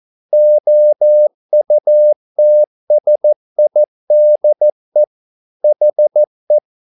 Morse code, heard as a keyed tone, about 14 wpm, 600 Hz. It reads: OUTSIDE HE
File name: mp3 files / a dogs mistake - 14wpm